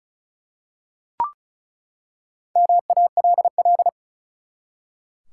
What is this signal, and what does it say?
Morse code, heard as a keyed tone, about 36 wpm, 700 Hz. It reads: MALL
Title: Four Letter Words - 35wpm